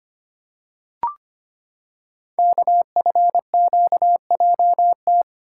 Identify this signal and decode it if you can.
Morse code, heard as a keyed tone, about 25 wpm, 700 Hz. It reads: KFQJT
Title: Code Groups - 5 Letters - 25wpm